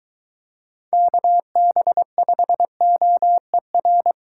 Morse code, heard as a keyed tone, about 23 wpm, 700 Hz. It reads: KB5OER